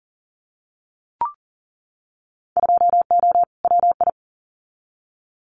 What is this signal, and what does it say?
Morse code, heard as a keyed tone, about 40 wpm, 700 Hz. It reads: 2O WI
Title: Winter Field Day Exchanges 40wpm